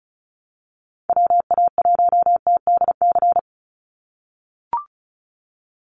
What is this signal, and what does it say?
Morse code, heard as a keyed tone, about 34 wpm, 700 Hz. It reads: WA1TDC
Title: USA Call Signs 35wpm